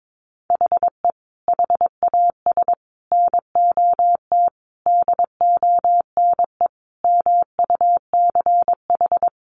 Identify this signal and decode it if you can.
Morse code, heard as a keyed tone, about 22 wpm, 700 Hz. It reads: HE HAS NOT DONE MUCH
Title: Sentences from Top 1100 Words 22wpm